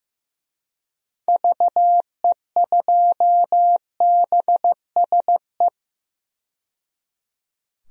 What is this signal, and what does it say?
Morse code, heard as a keyed tone, about 15 wpm, 700 Hz. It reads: VE2BSE
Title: DXCC-100 Call Signs - 6 Characters - 15wpm